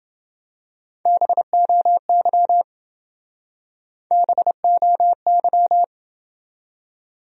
Morse code, heard as a keyed tone, about 27 wpm, 700 Hz. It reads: BOY BOY